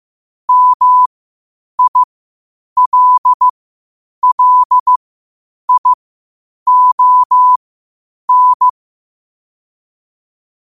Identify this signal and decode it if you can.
Morse code, heard as a keyed tone, about 15 wpm, spaced as for 8 wpm, 1000 Hz. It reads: MILLION
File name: qotdfiles / 8wpm